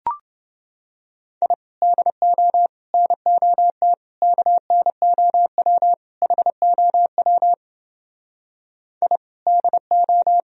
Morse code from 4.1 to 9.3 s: KNOW HOW I

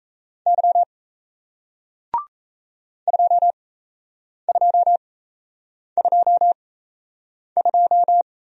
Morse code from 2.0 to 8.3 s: 2222

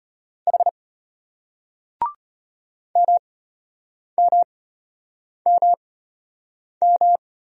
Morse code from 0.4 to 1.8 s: H

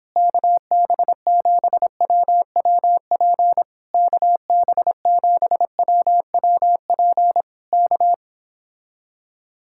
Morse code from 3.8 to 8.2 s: KB7WWP K

